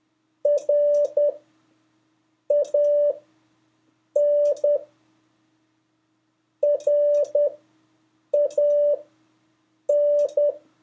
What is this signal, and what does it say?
Morse code, heard as a keyed tone, about 10 wpm, spaced as for 6 wpm, 580 Hz. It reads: RAN RAN
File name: 3of5-words-2